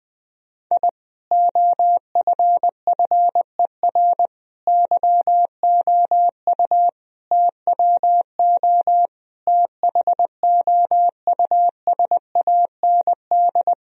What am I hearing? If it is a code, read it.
Morse code, heard as a keyed tone, about 21 wpm, 700 Hz. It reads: I OFFER YOU TWO THOUSAND
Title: Sentences from Top 1000 Words - Review Part 4 - 20wpm